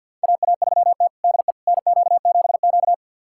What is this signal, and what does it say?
Morse code, heard as a keyed tone, about 48 wpm, 700 Hz. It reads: AA3T DE NQ7Q
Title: USA Callsign DE Callsign - 4 Characters - 50wpm